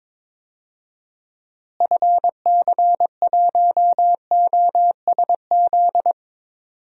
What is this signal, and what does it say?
Morse code, heard as a keyed tone, about 22 wpm, 700 Hz. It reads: FC1OSZ
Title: DXCC-100 Call Signs - 6 Characters - Repeated 3x - 22wpm